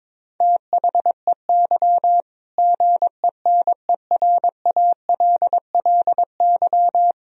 Morse code, heard as a keyed tone, about 22 wpm, 700 Hz. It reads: THEY GENERALLY
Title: Sentences from Top 1000 Words - Review Part 4 - 22wpm